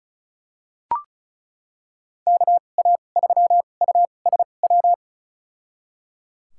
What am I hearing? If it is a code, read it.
Morse code, heard as a keyed tone, about 34 wpm, spaced as for 25 wpm, 700 Hz. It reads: KA3USW
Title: USA Call Signs - 6 Characters - 2x Character Spacing - 35wpm